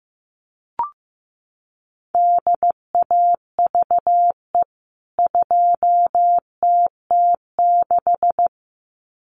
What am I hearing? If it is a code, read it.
Morse code, heard as a keyed tone, about 15 wpm, 700 Hz. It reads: DAVE 2TT6
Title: CWT Exchange 15wpm